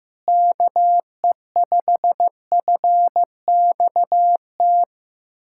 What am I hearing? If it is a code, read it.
Morse code, heard as a keyed tone, about 15 wpm, 700 Hz. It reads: KE5FXT
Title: USA Callsign DE Callsign - 6 Characters - 15wpm